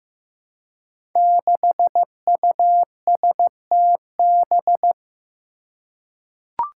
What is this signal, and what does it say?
Morse code, heard as a keyed tone, about 15 wpm, 700 Hz. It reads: 6USTB
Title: Five Challenging Letter-Number Combinations 15wpm